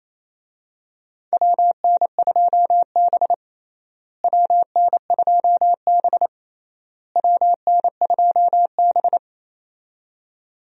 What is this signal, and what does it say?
Morse code, heard as a keyed tone, about 28 wpm, 700 Hz. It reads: WN2B WN2B WN2B
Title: USA Call Signs - 4 Characters - Repeated 3x - 28wpm